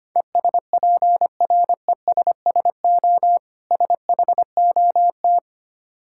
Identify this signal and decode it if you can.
Morse code, heard as a keyed tone, about 25 wpm, 700 Hz. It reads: ESPRESSO SHOT
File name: practice / 25wpm / Difficult Words with Hints - 25wpm